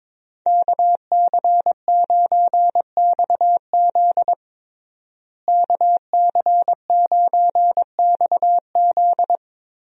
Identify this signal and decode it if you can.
Morse code, heard as a keyed tone, about 22 wpm, 700 Hz. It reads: KC9XZ KC9XZ
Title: USA Call Signs - 5 Characters - Repeated 2x - 22wpm